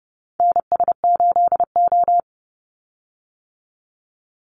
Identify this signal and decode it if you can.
Morse code, heard as a keyed tone, about 30 wpm, 700 Hz. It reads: NS8O
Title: Call Signs - 30wpm